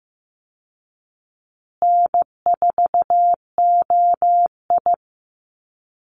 Morse code, heard as a keyed tone, about 16 wpm, 700 Hz. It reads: N4OI
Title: Easy Call Signs Repeated 3x - 15wpm